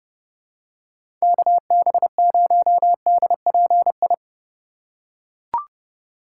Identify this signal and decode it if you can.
Morse code, heard as a keyed tone, about 30 wpm, 700 Hz. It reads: KB0DPI